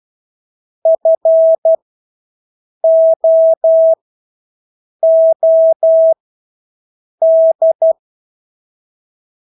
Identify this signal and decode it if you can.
Morse code, heard as a keyed tone, about 12 wpm, spaced as for 6 wpm, 650 Hz. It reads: FOOD
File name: Engelske ord 12-6 wpm